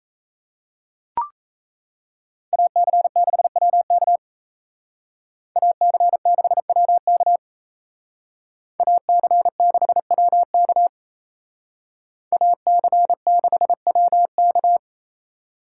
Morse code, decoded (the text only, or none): AC6WK AC6WK AC6WK AC6WK